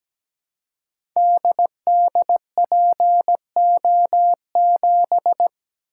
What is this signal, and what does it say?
Morse code, heard as a keyed tone, about 17 wpm, 690 Hz. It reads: DDPO7
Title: Code Groups - 5 Letter:Numbers - 17wpm